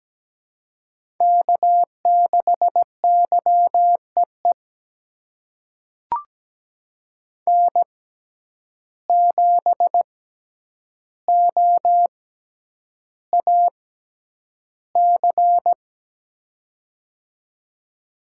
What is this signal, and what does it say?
Morse code, heard as a keyed tone, about 17 wpm, spaced as for 9 wpm, 700 Hz. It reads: K6YEE N7OAC
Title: USA Call Signs - 5 Characters - 6x Character Spacing - 17wpm